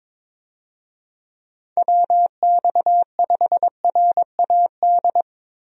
Morse code, heard as a keyed tone, about 22 wpm, 700 Hz. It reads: WX5RAD
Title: USA Call Signs - 6 Characters - Repeated 3x - 22wpm